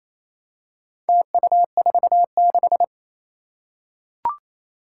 Morse code, heard as a keyed tone, about 27 wpm, 700 Hz. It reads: TU46
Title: Four Challenging Letter-Number Combinations 28wpm